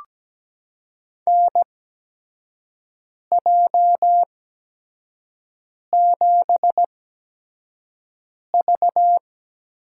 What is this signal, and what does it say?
Morse code, heard as a keyed tone, about 17 wpm, 700 Hz. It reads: N J 7 V
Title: USA Call Signs - 8x Character Spacing - 17wpm